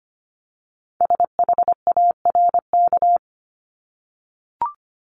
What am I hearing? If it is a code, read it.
Morse code, heard as a keyed tone, about 25 wpm, 700 Hz. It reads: SHARK
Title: Multi-letter Phonemes 25wpm